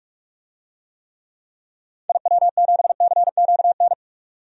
Morse code, heard as a keyed tone, about 46 wpm, 700 Hz. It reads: IW7CQN
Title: DXCC-100 Call Signs - 6 Characters - 45wpm